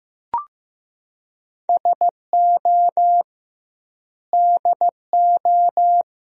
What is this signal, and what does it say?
Morse code, heard as a keyed tone, about 15 wpm, 700 Hz. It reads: SO DO
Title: 2x Word-Spacing - Sentences from Top 400 Words - 15wpm